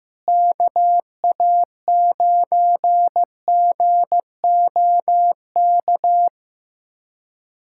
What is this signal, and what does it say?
Morse code, heard as a keyed tone, about 16 wpm, 700 Hz. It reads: KA9GOK
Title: USA Callsign DE Callsign - 6 Characters - 15wpm